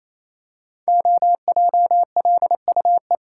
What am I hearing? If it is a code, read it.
Morse code, heard as a keyed tone, about 27 wpm, 700 Hz. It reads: OJLUE
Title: Code Groups - 5 Letter:Numbers - 28wpm